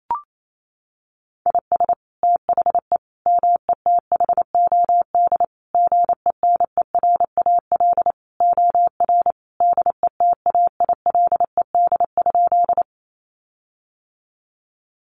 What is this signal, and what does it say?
Morse code, heard as a keyed tone, about 28 wpm, 700 Hz. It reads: IS THE METHOD GENERAL OR DETAILED?
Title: Sentences from Top 1500 Words 28wpm